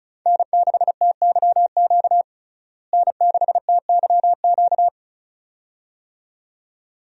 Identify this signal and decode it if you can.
Morse code, heard as a keyed tone, about 36 wpm, 700 Hz. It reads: N6TYQ N6TYQ
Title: USA Call Signs - 5 Characters - Repeated 2x - 35wpm